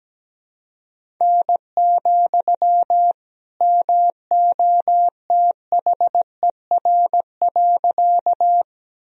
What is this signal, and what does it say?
Morse code, heard as a keyed tone, about 17 wpm, 700 Hz. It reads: N, MOTHER.
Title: Sentences from Top 200 Words and Abbreviations 17wpm